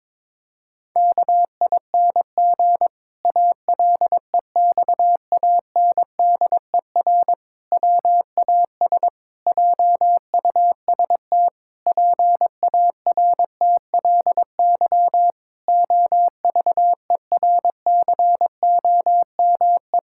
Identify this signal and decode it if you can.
Morse code, heard as a keyed tone, about 22 wpm, 700 Hz. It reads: KING ALEXANDER WAS JUST PARTLY OVERCOME
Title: Pangram Challenge v2 - 22wpm